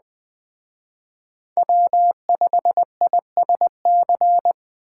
Morse code, heard as a keyed tone, about 20 wpm, 700 Hz. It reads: W5ISC
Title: Speed Racing - USA 1x3 Call Signs 17wpm